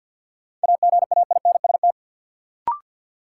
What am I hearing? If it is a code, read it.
Morse code, heard as a keyed tone, about 51 wpm, 700 Hz. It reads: AGAINST
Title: Sets of 2 Words - New Word Drill - 50wpm